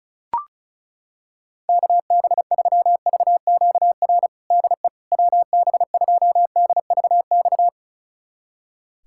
Morse code, heard as a keyed tone, about 36 wpm, 700 Hz. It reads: KB3VQR DE WB2DVX
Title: USA Callsign DE Callsign - 6 Characters - 35wpm